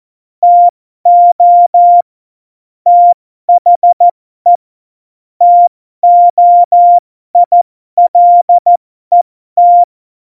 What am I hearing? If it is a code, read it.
Morse code, heard as a keyed tone, about 14 wpm, spaced as for 12 wpm, 700 Hz. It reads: TO THE TOILET
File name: Morsekurs / Witze